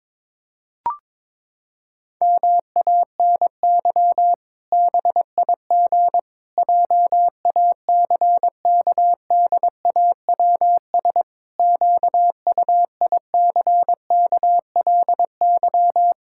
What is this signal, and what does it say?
Morse code, heard as a keyed tone, about 22 wpm, 700 Hz. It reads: MANY BIG JACKDAWS QUICKLY
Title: Pangram Challenge v2 - 22wpm